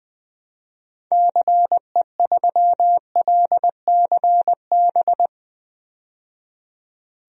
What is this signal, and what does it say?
Morse code, heard as a keyed tone, about 20 wpm, 700 Hz. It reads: CE3LCB
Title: DXCC-100 Call Signs - 6 Characters - 20wpm